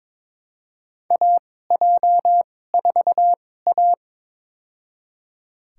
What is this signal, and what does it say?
Morse code, heard as a keyed tone, about 22 wpm, spaced as for 16 wpm, 700 Hz. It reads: AJ4A
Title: USA Call Signs - 4 Characters - 2x Character Spacing - 22wpm